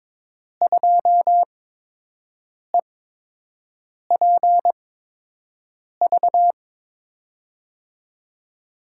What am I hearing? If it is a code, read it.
Morse code, heard as a keyed tone, about 22 wpm, spaced as for 6 wpm, 700 Hz. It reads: 2EPV
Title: USA Call Signs - 5 Characters - 8x Character Spacing - 22wpm